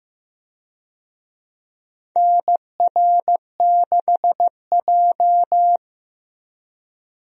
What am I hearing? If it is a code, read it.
Morse code, heard as a keyed tone, about 15 wpm, 700 Hz. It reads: NR6J
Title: USA Call Signs - 4 Characters - 15wpm